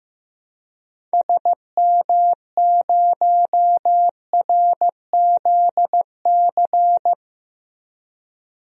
Morse code, heard as a keyed tone, about 15 wpm, 700 Hz. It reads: SM0RZC
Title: DXCC-100 Call Signs - 6 Characters - Repeated 2x - 15wpm